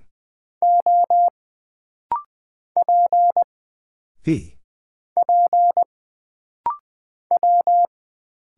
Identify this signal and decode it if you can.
Morse code, heard as a keyed tone, about 20 wpm, 700 Hz. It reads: O P P W
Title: Single Character (Letter:Number:Punctuation:Prosign) - Rapid-Fire - 20wpm